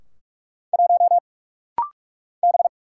Morse code, heard as a keyed tone, about 44 wpm, 700 Hz. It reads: 1 B
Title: Single Letter-Number - Rapid-Fire 45wpm